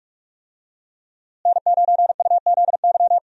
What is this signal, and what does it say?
Morse code, heard as a keyed tone, about 41 wpm, 700 Hz. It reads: N9UZY
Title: USA Call Signs - 5 Characters - Repeated 3x - 45wpm